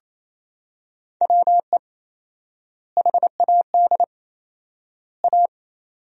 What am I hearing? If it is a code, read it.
Morse code, heard as a keyed tone, about 28 wpm, 700 Hz. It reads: WE HAD A
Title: 4x Word-Spacing - Sentences from Top 400 Words - 28wpm